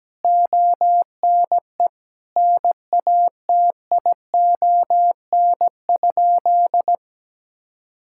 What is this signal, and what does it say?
Morse code, heard as a keyed tone, about 17 wpm, 700 Hz. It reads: ONE NATION?
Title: Sentences from Top 1000 Words - Review Part 6 - 17wpm